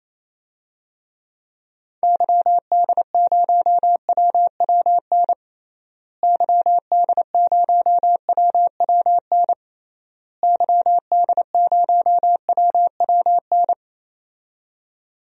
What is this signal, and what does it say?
Morse code, heard as a keyed tone, about 28 wpm, 700 Hz. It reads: YD0WWN YD0WWN YD0WWN
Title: DXCC-100 Call Signs - 6 Characters - Repeated 3x - 28wpm